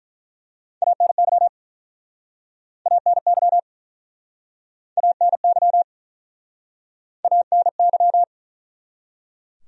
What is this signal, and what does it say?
Morse code, heard as a keyed tone, about 48 wpm, 700 Hz. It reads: ANY ANY ANY ANY